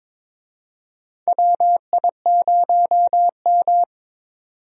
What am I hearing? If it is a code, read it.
Morse code, heard as a keyed tone, about 22 wpm, 700 Hz. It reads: WI0M
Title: USA Call Signs - 4 Characters - 22wpm